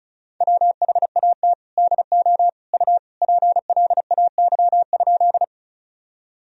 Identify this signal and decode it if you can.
Morse code, heard as a keyed tone, about 36 wpm, 700 Hz. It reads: WHAT DO U PLAY?
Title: Sentences from Top 200 Words and Abbreviations 35wpm